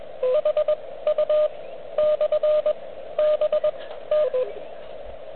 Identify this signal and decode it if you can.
Morse code, heard as a keyed tone, about 20 wpm, spaced as for 13 wpm, 600 Hz, with radio noise. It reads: HU/BT
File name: OK QRP 2022 / Zvuk / REC37